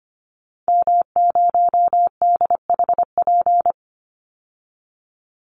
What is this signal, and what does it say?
Morse code, heard as a keyed tone, about 25 wpm, 700 Hz. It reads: M0DHP
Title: Call Signs - 25wpm